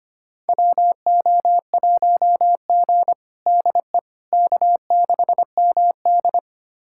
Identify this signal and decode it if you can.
Morse code, heard as a keyed tone, about 25 wpm, 700 Hz. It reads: WO1G DE K6MD
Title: USA Callsign DE Callsign - 4 Characters - 25wpm